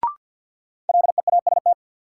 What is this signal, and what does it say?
Morse code, heard as a keyed tone, about 48 wpm, 700 Hz. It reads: LEAST
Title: Sentences from Top 700 Words - New Word Drill - 50wpm